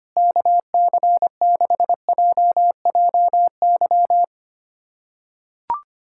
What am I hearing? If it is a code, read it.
Morse code, heard as a keyed tone, about 25 wpm, 700 Hz. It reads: KC6JJY
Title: USA Callsign DE Callsign 25wpm